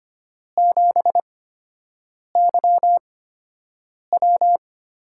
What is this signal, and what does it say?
Morse code, heard as a keyed tone, about 25 wpm, spaced as for 7 wpm, 700 Hz. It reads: 7YW
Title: USA Call Signs - 5 Characters - 8x Character Spacing - 25wpm